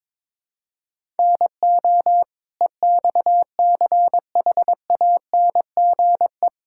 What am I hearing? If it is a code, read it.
Morse code, heard as a keyed tone, about 22 wpm, 700 Hz. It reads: NO EXCHANGE
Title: Sentences from Top 1000 Words - Review Part 6 - 22wpm